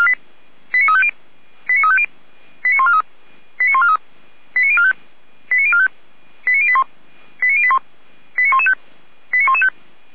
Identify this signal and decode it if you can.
Five-tone selective call (ZVEI), heard as quick runs of five tones, tones of about 70 ms. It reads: 89350 89350 89143 89143 89045 89045 89081 89081 89115 89115